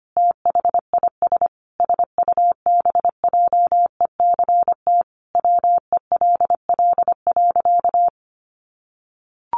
Morse code, heard as a keyed tone, about 25 wpm, 700 Hz. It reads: THIS SUBJECT WELL.
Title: Sentences from Top 800 Words 25wpm